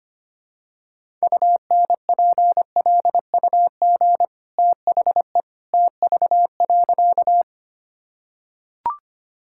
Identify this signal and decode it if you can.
Morse code, heard as a keyed tone, about 25 wpm, 700 Hz. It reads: UNPLUG THE TV.
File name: Easy Sentences with Hints - 25wpm